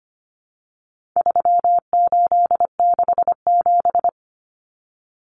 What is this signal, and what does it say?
Morse code, heard as a keyed tone, about 25 wpm, 700 Hz. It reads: 3867
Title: Numbers - 4-Digit 25wpm